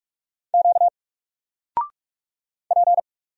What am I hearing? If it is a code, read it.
Morse code, heard as a keyed tone, about 44 wpm, 700 Hz. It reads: Q P
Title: Single Character (Letter:Number:Punctuation:Prosign) 45wpm